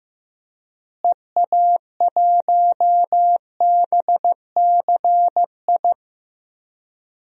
Morse code, heard as a keyed tone, about 15 wpm, 700 Hz. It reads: EA1BCI
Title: DXCC-100 Call Signs - 6 Characters - Repeated 3x - 15wpm